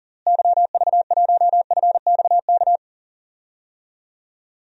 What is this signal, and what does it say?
Morse code, heard as a keyed tone, about 40 wpm, 700 Hz. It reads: YV1FXK
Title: DXCC-100 Call Signs - 6 Characters - Repeated 3x - 40wpm